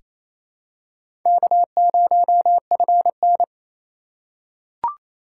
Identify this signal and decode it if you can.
Morse code, heard as a keyed tone, about 27 wpm, 700 Hz. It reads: K0FN
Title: USA Call Signs - 4 Characters - 4x Character Spacing - 28wpm